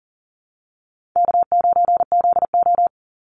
Morse code, heard as a keyed tone, about 40 wpm, 700 Hz. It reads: K9ZO
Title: Call Signs - 40wpm